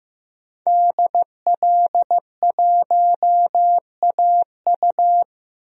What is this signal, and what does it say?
Morse code, heard as a keyed tone, about 15 wpm, 700 Hz. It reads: DL1AU
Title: DXCC-100 Call Signs - 5 Characters - 15wpm